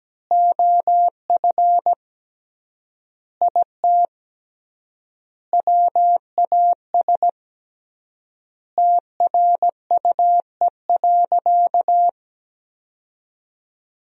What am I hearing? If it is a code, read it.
Morse code, heard as a keyed tone, about 17 wpm, 700 Hz. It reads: OF IT WAS TRUE.
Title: 3x Word-Spacing - Sentences from Top 400 Words - 17wpm